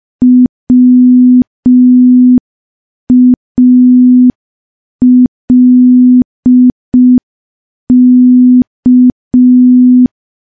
Morse code, heard as a keyed tone, about 5 wpm, 250 Hz. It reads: WALK
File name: morse / morse-training-05WPM